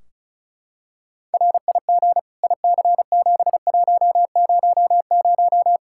Morse code, heard as a keyed tone, about 34 wpm, 700 Hz. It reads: RIG IC7100
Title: K1USN SST QSO Practice 35wpm